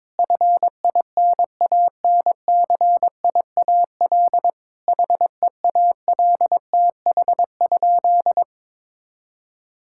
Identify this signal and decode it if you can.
Morse code, heard as a keyed tone, about 22 wpm, 700 Hz. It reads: FINANCIAL HEALTH?
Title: Sentences from Top 1000 Words - Review Part 5 - 22wpm